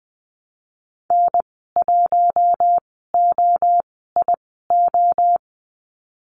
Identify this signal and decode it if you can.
Morse code, heard as a keyed tone, about 20 wpm, spaced as for 14 wpm, 700 Hz. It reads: N1OIO
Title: 082 Lesson 14g - Number 2 - Call Signs 20wpm